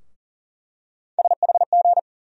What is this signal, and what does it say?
Morse code, heard as a keyed tone, about 40 wpm, 700 Hz. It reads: SHG
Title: USA Call Sign Postfixes - 3 Letters 40wpm